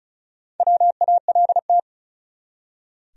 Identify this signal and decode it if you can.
Morse code, heard as a keyed tone, about 35 wpm, 700 Hz. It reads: WALT
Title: USA Common Names - Max 4 Letters - 35wpm